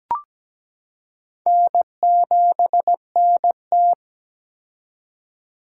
Morse code, heard as a keyed tone, about 17 wpm, 700 Hz. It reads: N7NT
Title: K1USN SST QSO Practice - 17wpm